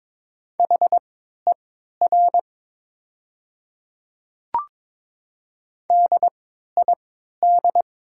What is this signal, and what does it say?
Morse code, heard as a keyed tone, about 22 wpm, spaced as for 13 wpm, 700 Hz. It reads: HER DID
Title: 3x Character Spacing - Top 100 Words - 22wpm